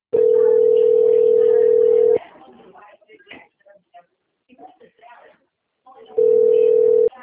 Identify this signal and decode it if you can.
Ringback tone (North America), 440 + 480 Hz, with speech.
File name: out- - - - - .10164